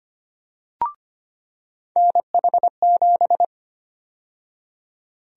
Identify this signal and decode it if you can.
Morse code, heard as a keyed tone, about 25 wpm, 700 Hz. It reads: NH7
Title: USA Call Sign Prefixes - 25wpm